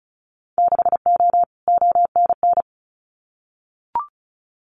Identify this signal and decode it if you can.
Morse code, heard as a keyed tone, about 34 wpm, 700 Hz. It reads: 6O ONN
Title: Winter Field Day Exchanges Encore 35wpm